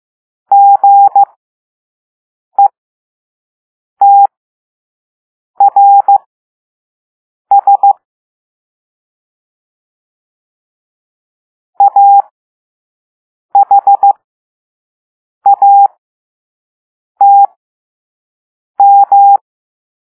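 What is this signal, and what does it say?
Morse code, heard as a keyed tone, about 15 wpm, spaced as for 5 wpm, 800 Hz. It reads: GETRS AHATM
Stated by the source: Morse code lesson 5 drill of random five-letter groups using all letters learned so far.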